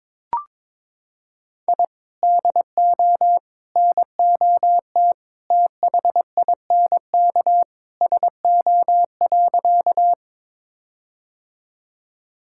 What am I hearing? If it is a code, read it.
Morse code, heard as a keyed tone, about 22 wpm, 700 Hz. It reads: I DO NOT THINK SO.
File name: practice / 22wpm / Sentences with 4-Word Phrases - 22wpm